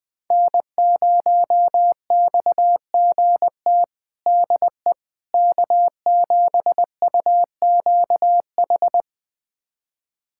Morse code, heard as a keyed tone, about 20 wpm, 700 Hz. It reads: N0XGT DE K7UQH
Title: USA Callsign DE Callsign - 5 Characters - 20wpm